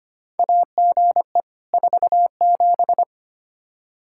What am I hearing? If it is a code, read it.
Morse code, heard as a keyed tone, about 25 wpm, 700 Hz. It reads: AGE 47
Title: K1USN SST QSO Practice 25wpm